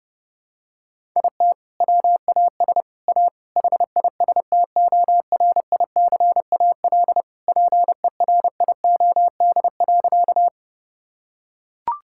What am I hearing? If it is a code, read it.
Morse code, heard as a keyed tone, about 32 wpm, 700 Hz. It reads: IT WAS A HISTORICAL PERIOD.
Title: Sentences from Top 1000 Words - Review Part 5 - 30wpm